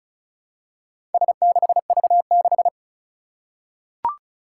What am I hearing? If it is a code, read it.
Morse code, heard as a keyed tone, about 34 wpm, 700 Hz. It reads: S6V6
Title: Four Challenging Letter-Number Combinations 35wpm